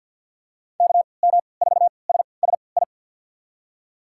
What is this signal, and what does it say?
Morse code, heard as a keyed tone, about 47 wpm, spaced as for 28 wpm, 700 Hz. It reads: KM4SSI
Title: USA Call Signs - 6 Characters - 3x Character Spacing - 50wpm